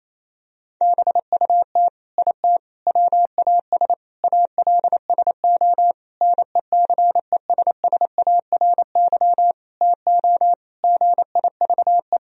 Morse code, heard as a keyed tone, about 28 wpm, 700 Hz. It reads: BUT IT WAS ALSO NECESSARY TO GIVE
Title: Sentences from Top 1000 Words - Review Part 2 - 28wpm